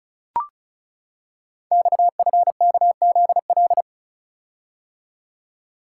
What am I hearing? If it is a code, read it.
Morse code, heard as a keyed tone, about 36 wpm, 700 Hz. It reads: XFKZL
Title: Code Groups - 5 Letters - 35wpm